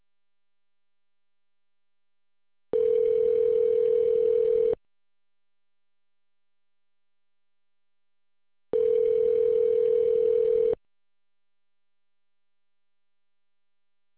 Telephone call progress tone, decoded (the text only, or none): ringback tone